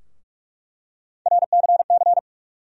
Morse code, heard as a keyed tone, about 42 wpm, 700 Hz. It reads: RCC
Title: USA Call Sign Postfixes - 3 Letters 45wpm